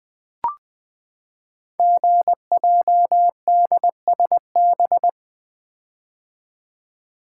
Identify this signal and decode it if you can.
Morse code, heard as a keyed tone, about 20 wpm, 700 Hz. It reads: GJDSB
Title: Code Groups - 5 Letters - 20wpm